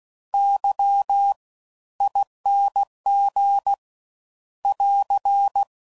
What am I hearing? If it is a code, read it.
Morse code, heard as a keyed tone, about 16 wpm, spaced as for 13 wpm, 800 Hz. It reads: YING <AR>
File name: Suffixes / Special INGs / Fast Ings